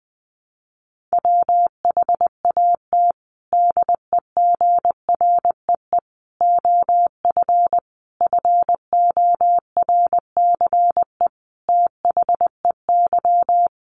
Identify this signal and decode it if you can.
Morse code, heard as a keyed tone, about 19 wpm, 700 Hz. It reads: WHAT DEGREE OF FORCE THEY
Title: Sentences from Top 1500 Words 20wpm